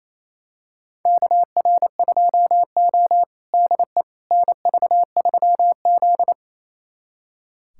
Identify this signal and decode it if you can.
Morse code, heard as a keyed tone, about 28 wpm, 700 Hz. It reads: KR2O DE NV3Z
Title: USA Callsign DE Callsign - 4 Characters - 28wpm